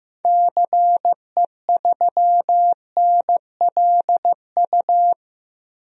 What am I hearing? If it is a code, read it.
Morse code, heard as a keyed tone, about 15 wpm, 700 Hz. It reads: CE3NLU
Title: DXCC-100 Call Signs - 6 Characters - 15wpm